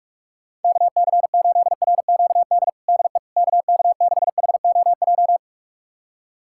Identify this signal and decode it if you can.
Morse code, heard as a keyed tone, about 42 wpm, 700 Hz. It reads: KC8RQD DE KK6HOJ
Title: USA Callsign DE Callsign - 6 Characters - 45wpm